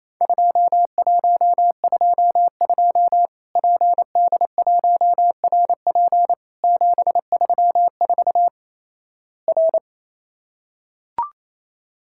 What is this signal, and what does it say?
Morse code, heard as a keyed tone, about 28 wpm, 700 Hz. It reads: 2122 PD1RP 734 R
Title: Worked All Europe DX Contest - QTC Exchange - 28wpm